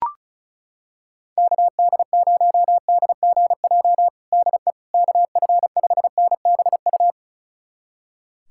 Morse code, heard as a keyed tone, about 36 wpm, 700 Hz. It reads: KD0DGJ DE KF5NBU
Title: USA Callsign DE Callsign - 6 Characters - 35wpm